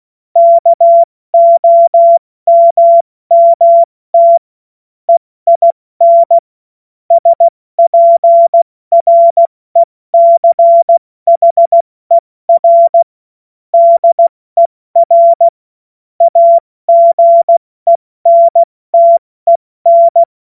Morse code, heard as a keyed tone, about 16 wpm, 680 Hz. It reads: KOMMT EIN SPRECHER DER AGENTEN